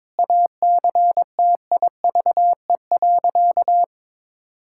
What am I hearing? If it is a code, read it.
Morse code, heard as a keyed tone, about 21 wpm, 700 Hz. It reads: ACTIVE.
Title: Sentences from Top 1000 Words - Review Part 6 - 22wpm